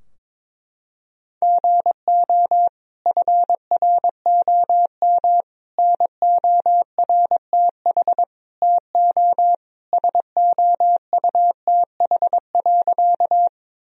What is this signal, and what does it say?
Morse code, heard as a keyed tone, about 22 wpm, 700 Hz. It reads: GO FROM NORTH TO SOUTH.
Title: Sentences from Top 1000 Words - Review Part 2 - 22wpm